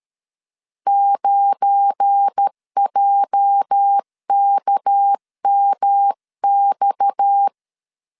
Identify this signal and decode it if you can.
Morse code, heard as a keyed tone, about 13 wpm, 800 Hz. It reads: 9JKMX
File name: CW / DET / 1 9 1500C S A 12wpm 3800